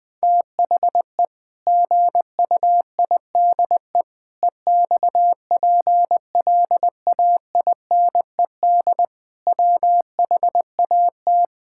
Morse code, heard as a keyed tone, about 20 wpm, 700 Hz. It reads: THE GUIDE EXPLAINED WHAT
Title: Sentences with 8-Word Phrases - Spoken Beginning - 20wpm